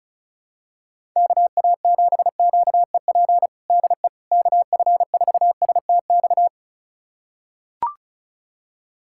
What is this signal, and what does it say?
Morse code, heard as a keyed tone, about 36 wpm, 700 Hz. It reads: KA7QEP DE KF4STX
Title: USA Callsign DE Callsign 35wpm